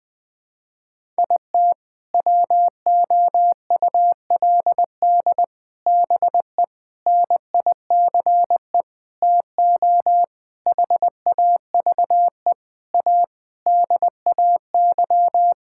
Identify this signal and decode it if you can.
Morse code, heard as a keyed tone, about 20 wpm, 700 Hz. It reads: IT WOULD BE NICE TO HAVE A DAY
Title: Sentences with 7-Word Phrases - Spoken Beginning - 20wpm